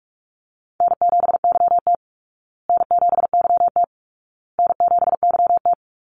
Morse code, heard as a keyed tone, about 44 wpm, 700 Hz. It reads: N7YT N7YT N7YT
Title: Call Signs Repeated 2x - 45wpm